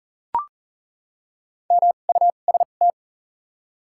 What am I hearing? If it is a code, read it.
Morse code, heard as a keyed tone, about 40 wpm, spaced as for 29 wpm, 700 Hz. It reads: MUST
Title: 2x Character Spacing - Top 300 Words - 40wpm